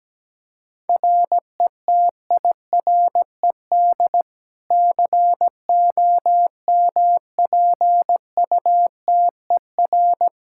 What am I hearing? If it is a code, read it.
Morse code, heard as a keyed tone, about 17 wpm, 700 Hz. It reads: RETIRED COMPUTER
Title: Practice QSO - Exchange 3 - More Info - 17wpm